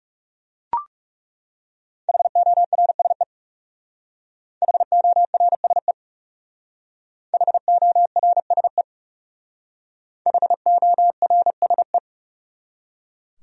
Morse code, heard as a keyed tone, about 40 wpm, 700 Hz. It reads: HORSE HORSE HORSE HORSE